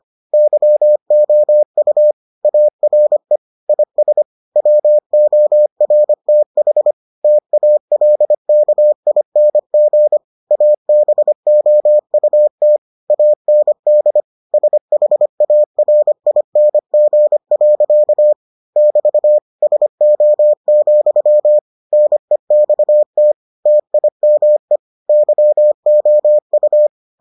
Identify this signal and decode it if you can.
Morse code, heard as a keyed tone, about 25 wpm, 600 Hz. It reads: YOU ARE IS WORTH TALKING ABOUT AND SHARING. <BT> SO, NEXT TIME YOU